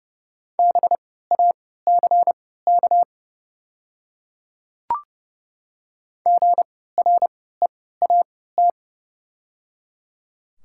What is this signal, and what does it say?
Morse code, heard as a keyed tone, about 30 wpm, spaced as for 17 wpm, 700 Hz. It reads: BACK GREAT
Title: 3x Character Spacing - Top 200 Words - 30wpm